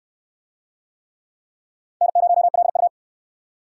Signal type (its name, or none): Morse code